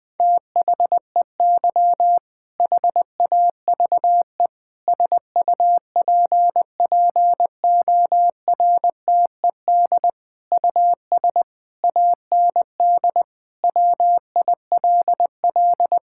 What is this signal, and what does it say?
Morse code, heard as a keyed tone, about 20 wpm, 700 Hz. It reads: THEY HAVE SUPPORTED US AND WILL
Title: Sentences with 6-Word Phrases - 20wpm